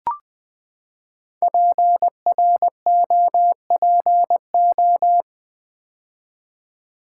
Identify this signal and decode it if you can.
Morse code, heard as a keyed tone, about 20 wpm, 700 Hz. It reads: PROPO
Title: Build A Word - 5 Letters At A Time - 10-Letter Words - 20wpm